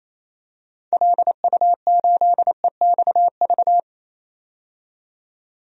Morse code, heard as a keyed tone, about 28 wpm, 700 Hz. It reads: LU8EXV
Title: DXCC-100 Call Signs - 6 Characters - 28wpm